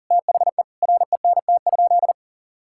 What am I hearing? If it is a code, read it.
Morse code, heard as a keyed tone, about 40 wpm, 700 Hz. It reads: THE RENT?
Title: Sentences from Top 1000 Words - Review Part 5 - 40wpm